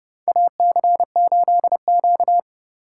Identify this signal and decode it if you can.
Morse code, heard as a keyed tone, about 30 wpm, 700 Hz. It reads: AC8Q
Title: USA Call Signs - 4 Characters - 2x Character Spacing - 30wpm